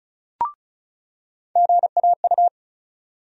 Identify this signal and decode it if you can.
Morse code, heard as a keyed tone, about 34 wpm, 700 Hz. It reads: GAU
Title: USA Call Sign Postfixes - 3 Letters 35wpm